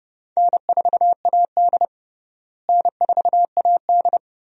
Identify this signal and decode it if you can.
Morse code, heard as a keyed tone, about 29 wpm, 700 Hz. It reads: N4AD N4AD
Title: USA Call Signs - 4 Characters - Repeated 3x - 30wpm